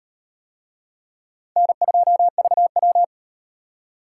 Morse code, heard as a keyed tone, about 36 wpm, 700 Hz. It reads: N2VW